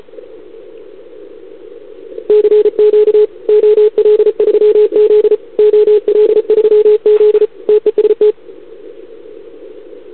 Morse code, heard as a keyed tone, about 36 wpm, 440 Hz. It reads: CQ OL3Z OL3Z TEST
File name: CQWWCW2021 / OL3Z